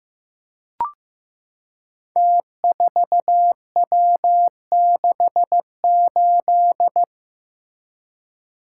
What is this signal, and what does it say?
Morse code, heard as a keyed tone, about 15 wpm, 700 Hz. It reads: T4W68
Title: Code Groups - 5 Letter:Numbers - 15wpm